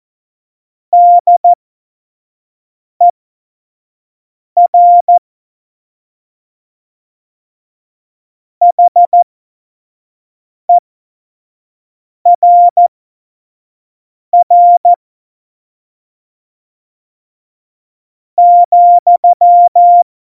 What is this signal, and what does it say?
Morse code, heard as a keyed tone, about 14 wpm, spaced as for 5 wpm, 700 Hz. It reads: DER HERR ,
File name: Morsekurs / Witze / Witz 13 March 2018, 25 F70